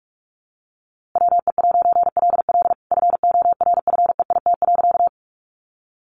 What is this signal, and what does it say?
Morse code, heard as a keyed tone, about 44 wpm, 700 Hz. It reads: WE'LL FORFEIT.
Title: A Sound Of Thunder 45wpm